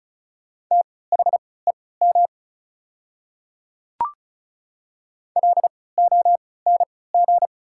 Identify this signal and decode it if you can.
Morse code, heard as a keyed tone, about 36 wpm, spaced as for 20 wpm, 690 Hz. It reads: THEM LONG
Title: 3x Character Spacing - Top 100 Words - 35wpm